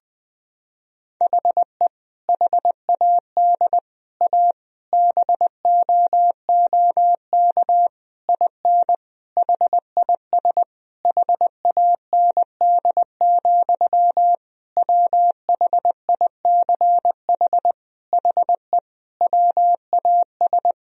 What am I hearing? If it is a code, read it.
Morse code, heard as a keyed tone, about 21 wpm, 700 Hz. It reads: HE HAD A BOOK IN HIS HAND, WHICH HE WAS
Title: Sentences from Top 1000 Words - Review Part 3 - 20wpm